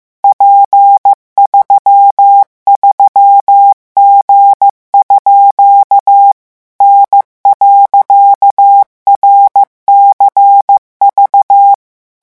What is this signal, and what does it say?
Morse code, heard as a keyed tone, about 15 wpm, 800 Hz. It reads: P33G_ N.RCV
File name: morse mp3 / revision14-vitesse 14 mots